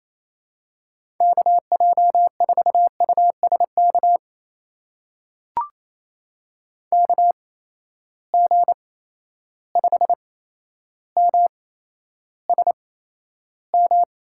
Morse code, heard as a keyed tone, about 28 wpm, spaced as for 12 wpm, 700 Hz. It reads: KJ4USK KG5MSM
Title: USA Call Signs - 6 Characters - 8x Character Spacing - 28wpm